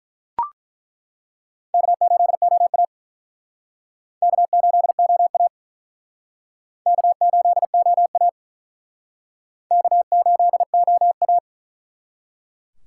Morse code, heard as a keyed tone, about 48 wpm, 700 Hz. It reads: K8OA K8OA K8OA K8OA